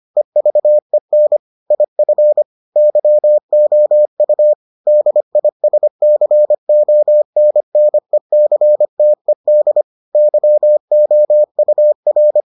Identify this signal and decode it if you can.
Morse code, heard as a keyed tone, about 25 wpm, 600 Hz. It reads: EVEN IF YOU DISCONNECTED YOUR